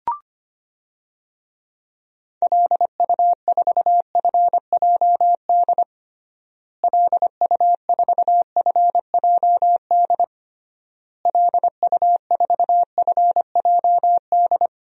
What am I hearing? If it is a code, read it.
Morse code, heard as a keyed tone, about 25 wpm, 700 Hz. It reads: LU4FJD LU4FJD LU4FJD
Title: DXCC-100 Call Signs - 6 Characters - Repeated 3x - 25wpm